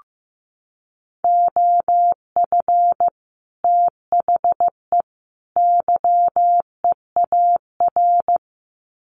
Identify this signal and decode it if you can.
Morse code, heard as a keyed tone, about 15 wpm, 700 Hz. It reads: OF THE YEAR
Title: Sets of 3 Words 15wpm